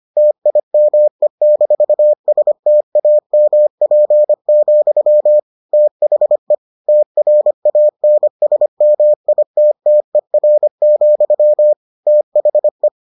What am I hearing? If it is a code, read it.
Morse code, heard as a keyed tone, about 25 wpm, 600 Hz. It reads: TIME-STAMP, THE TRANSMITTER, THE